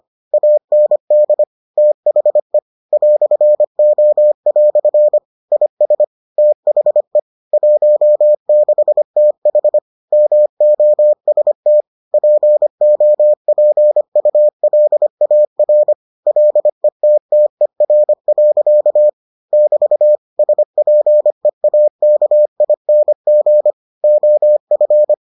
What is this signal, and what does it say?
Morse code, heard as a keyed tone, about 25 wpm, 600 Hz. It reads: AND THE "O" IS THE 16TH MOST POPULAR LETTER. <BT> SPEAKING OF